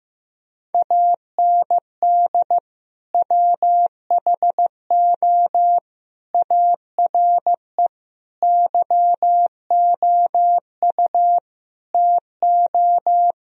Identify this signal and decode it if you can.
Morse code, heard as a keyed tone, about 15 wpm, 700 Hz. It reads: AND WHO ARE YOU TO
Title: Sentences from Top 1000 Words - Review Part 2 - 15wpm